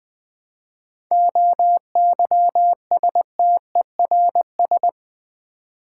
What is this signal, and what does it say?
Morse code, heard as a keyed tone, about 20 wpm, 700 Hz. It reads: OYSTERS
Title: Difficult Words with Hints - 20wpm